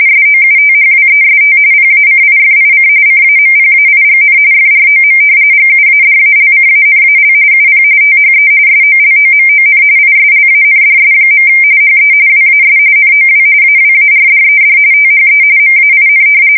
RTTY, about 45.45 baud. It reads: SOUTHWEST PACIFIC CAMPAIGN FROM GUADALCANAL UP THE SOLOMON ISLANDS. HE WAS NOW AWARDED THE MAIN
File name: fsk170 / battleshipnj